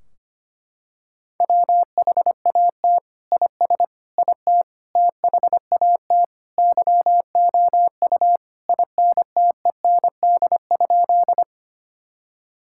Morse code, heard as a keyed tone, about 25 wpm, 700 Hz. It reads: WHAT IS IT THAT YOU INTEND?
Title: Sentences from Top 1600 Words 25wpm